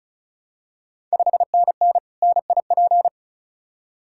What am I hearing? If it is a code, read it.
Morse code, heard as a keyed tone, about 36 wpm, 700 Hz. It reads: 5NN NIP
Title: Ontario QSO Party Exchange 35wpm